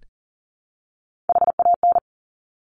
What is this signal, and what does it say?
Morse code, heard as a keyed tone, about 42 wpm, 700 Hz. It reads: HAN
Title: Top Three Letter Combinations in Words 40wpm